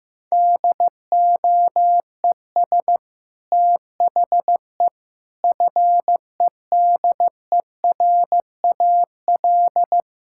Morse code, heard as a keyed tone, about 15 wpm, 700 Hz. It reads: DOES THE FEDERAL
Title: Sentences from Top 1000 Words - Review Part 5 - 15wpm